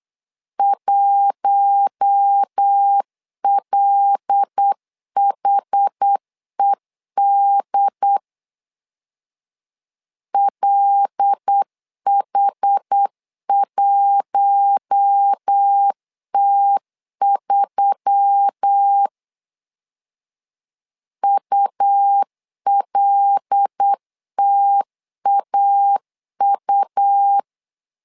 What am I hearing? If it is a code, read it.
Morse code, heard as a keyed tone, about 8 wpm, 800 Hz. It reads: 1LHED LH1T3 ULTAU
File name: LEZIONI / 5 LBVC / 5 L 8wpm 2831